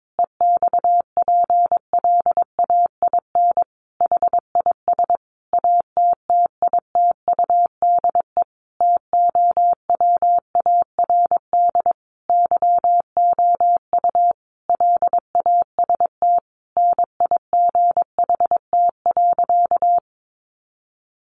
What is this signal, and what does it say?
Morse code, heard as a keyed tone, about 22 wpm, 700 Hz. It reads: EXPLAIN HIS ATTITUDE TOWARD YOU LAST NIGHT.
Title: Sentences from Top 1300 Words 22wpm